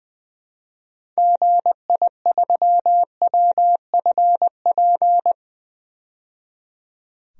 Morse code, heard as a keyed tone, about 20 wpm, 700 Hz. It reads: GI3WFP
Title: DXCC-100 Call Signs - 6 Characters - 20wpm